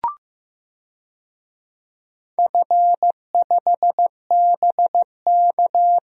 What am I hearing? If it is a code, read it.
Morse code, heard as a keyed tone, about 15 wpm, 700 Hz. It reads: F5BK
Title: DXCC-100 Call Signs - 4 Characters - 15wpm